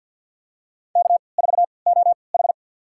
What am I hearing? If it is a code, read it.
Morse code, heard as a keyed tone, about 53 wpm, spaced as for 29 wpm, 700 Hz. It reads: K4OH
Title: USA Call Signs - 3x Character Spacing - 50wpm